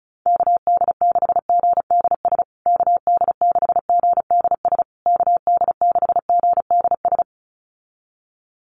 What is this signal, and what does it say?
Morse code, heard as a keyed tone, about 34 wpm, 700 Hz. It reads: KD6GDS KD6GDS KD6GDS
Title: USA Call Signs Repeated 3x 35wpm